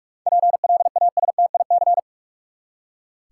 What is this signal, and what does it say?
Morse code, heard as a keyed tone, about 46 wpm, 700 Hz. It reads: PLASTIC
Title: Seven Letter Words - 45wpm